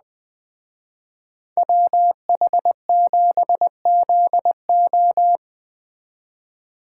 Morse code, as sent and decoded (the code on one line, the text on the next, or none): .-- .... --... --.. ---
WH7ZO